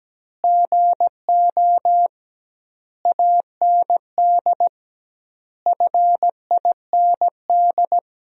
Morse code, heard as a keyed tone, about 17 wpm, 700 Hz. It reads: GO AND FIND
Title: 2x Word-Spacing - Sentences from Top 200 Words - 17wpm